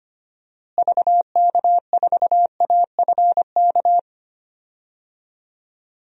Morse code, heard as a keyed tone, about 25 wpm, 700 Hz. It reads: VK4AFK
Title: DXCC-100 Call Signs - 6 Characters - Repeated 3x - 25wpm